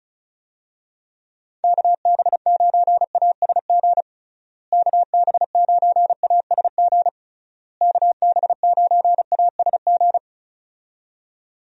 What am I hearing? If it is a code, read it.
Morse code, heard as a keyed tone, about 34 wpm, 700 Hz. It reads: KB9ASG KB9ASG KB9ASG
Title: USA Call Signs - 6 Characters - Repeated 3x - 35wpm